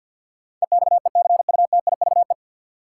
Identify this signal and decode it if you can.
Morse code, heard as a keyed tone, about 53 wpm, 700 Hz. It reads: EXECUTIVE
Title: Sets of 5 Words Encore - New Word Drill - 50wpm